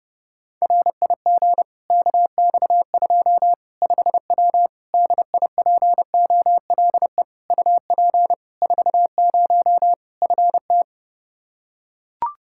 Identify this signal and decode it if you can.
Morse code, heard as a keyed tone, about 30 wpm, 700 Hz. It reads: RIG KX2 5W DIPOLE UP 40 FT
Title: Practice QSO - Exchange 2 - Info - 30wpm